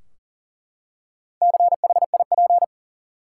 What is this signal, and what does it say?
Morse code, heard as a keyed tone, about 40 wpm, 700 Hz. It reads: CHIP
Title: Four Letter Words - 40wpm